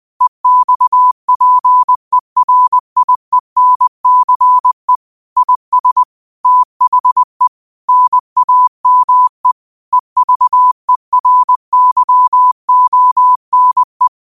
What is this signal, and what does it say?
Morse code, heard as a keyed tone, about 20 wpm, 1000 Hz. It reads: EXPERIENCE IS THE NAME EVERYONE
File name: qotdfiles / 20wpm